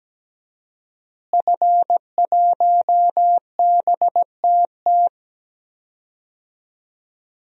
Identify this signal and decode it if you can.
Morse code, heard as a keyed tone, about 17 wpm, 700 Hz. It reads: F1BTT
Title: DXCC-100 Call Signs - 5 Characters - Repeated 3x - 17wpm